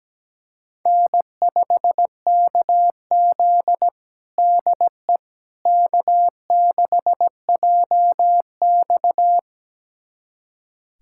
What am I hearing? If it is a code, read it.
Morse code, heard as a keyed tone, about 17 wpm, 700 Hz. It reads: N5KZ DE K6JX
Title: USA Callsign DE Callsign - 4 Characters - 17wpm